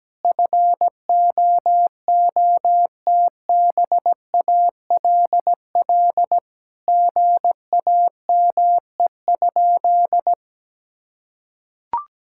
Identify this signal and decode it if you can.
Morse code, heard as a keyed tone, about 17 wpm, 700 Hz. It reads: FOOTBALL GAME?
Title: Sentences from Top 1600 Words 17wpm